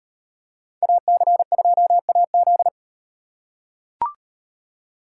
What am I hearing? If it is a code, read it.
Morse code, heard as a keyed tone, about 38 wpm, 700 Hz. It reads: AC2AZ